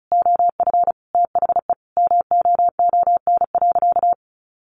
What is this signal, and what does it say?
Morse code, heard as a keyed tone, about 34 wpm, 700 Hz. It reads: OF THE MOON.
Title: Sentences from Top 1500 Words 35wpm